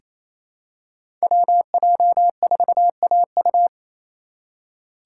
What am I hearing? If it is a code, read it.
Morse code, heard as a keyed tone, about 27 wpm, 690 Hz. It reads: WJ4AU
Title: Code Groups - 5 Letter:Numbers - 28wpm